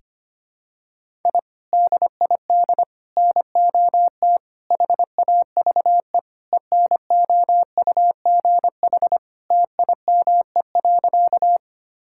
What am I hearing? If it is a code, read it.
Morse code, heard as a keyed tone, about 25 wpm, 700 Hz. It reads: I DID NOT HAVE ENOUGH TIME.
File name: practice / 25wpm / Sentences with 4-Word Phrases - 25wpm